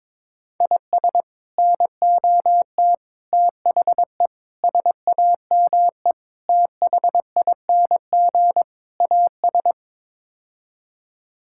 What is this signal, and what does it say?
Morse code, heard as a keyed tone, about 22 wpm, 700 Hz. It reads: IS NOT THE SAME THING AS
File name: practice / 22wpm / Sets of 6 Words - 22wpm